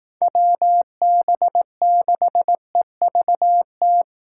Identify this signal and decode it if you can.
Morse code, heard as a keyed tone, about 18 wpm, 700 Hz. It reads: WB6EVT
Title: Speed Racing - USA 2x3 Call Signs 15wpm